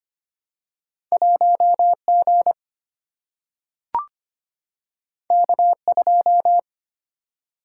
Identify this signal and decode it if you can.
Morse code, heard as a keyed tone, about 25 wpm, 700 Hz. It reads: 1G K2
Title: Single-Letter Single-Number 25wpm